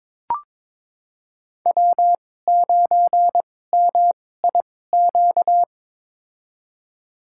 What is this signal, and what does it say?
Morse code, heard as a keyed tone, about 22 wpm, spaced as for 16 wpm, 690 Hz. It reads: W9MIQ
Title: USA Call Signs - 5 Characters - 2x Character Spacing - 22wpm